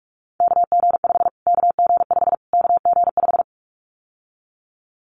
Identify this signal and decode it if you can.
Morse code, heard as a keyed tone, about 44 wpm, 700 Hz. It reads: KG5 KG5 KG5
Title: Callsign Prefixes for Top 25 DXCC - Repeated 3x - 45wpm